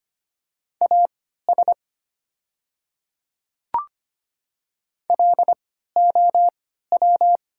Morse code, heard as a keyed tone, about 25 wpm, spaced as for 14 wpm, 700 Hz. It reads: AS LOW
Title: 3x Character Spacing - Top 200 Words - 25wpm